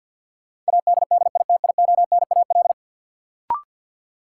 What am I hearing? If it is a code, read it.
Morse code, heard as a keyed tone, about 48 wpm, 700 Hz. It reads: ADDITIONAL
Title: Sentences from Top 1400 Words - New Word Drill Alt - 50wpm